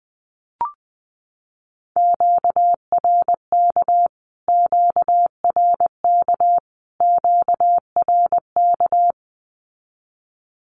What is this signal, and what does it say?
Morse code, heard as a keyed tone, about 20 wpm, 700 Hz. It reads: QRK QRK QRK
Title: Q-Codes Repeated 3x 20wpm